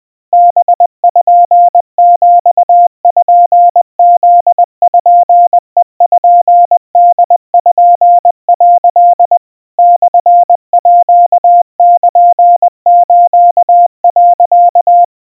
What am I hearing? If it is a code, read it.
Morse code (Japanese Wabun code), heard as a keyed tone, about 21 wpm, 700 Hz. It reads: パピプペポ」 モールス、
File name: kawa / Morse sample / aiueo 70 cpm 700 Hz Crispy